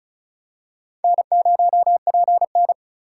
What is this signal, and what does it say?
Morse code, heard as a keyed tone, about 36 wpm, 700 Hz. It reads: N0PN
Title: USA Call Signs - 4 Characters - 3x Character Spacing - 35wpm